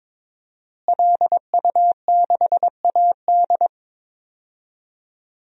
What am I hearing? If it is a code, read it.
Morse code, heard as a keyed tone, about 21 wpm, 700 Hz. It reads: LU6AD
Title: DXCC-100 Call Signs - 5 Characters - Repeated 3x - 22wpm